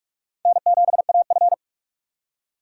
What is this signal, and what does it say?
Morse code, heard as a keyed tone, about 47 wpm, 700 Hz. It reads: N7AF
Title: USA Call Signs - 4 Characters - 4x Character Spacing - 45wpm